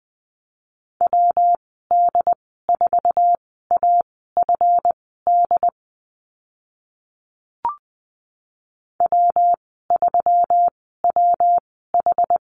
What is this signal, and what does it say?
Morse code, heard as a keyed tone, about 20 wpm, spaced as for 14 wpm, 700 Hz. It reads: WD4AFD W3WH
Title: 140 Lesson 23g - Letter f - Call Signs 20wpm